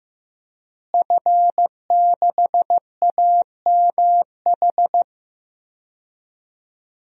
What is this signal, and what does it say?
Morse code, heard as a keyed tone, about 15 wpm, 700 Hz. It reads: F6AMH
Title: DXCC-100 Call Signs - 5 Characters - Repeated 2x - 15wpm